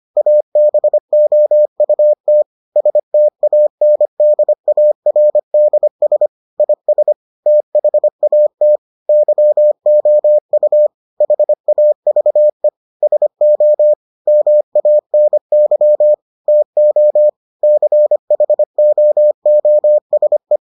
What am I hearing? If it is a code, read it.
Morse code, heard as a keyed tone, about 25 wpm, 600 Hz. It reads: ABOUT STANDARDS IS THAT YOU HAVE SO MANY TO CHOOSE